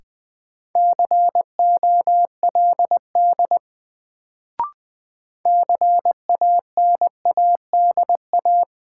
Morse code, heard as a keyed tone, about 20 wpm, 700 Hz. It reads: COLD CANADA
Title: Sets of 5 Words - New Word Drill - 20wpm